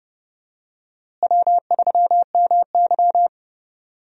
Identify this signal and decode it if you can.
Morse code, heard as a keyed tone, about 31 wpm, 700 Hz. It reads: W3MY
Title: USA Call Signs - 4 Characters - Repeated 3x - 30wpm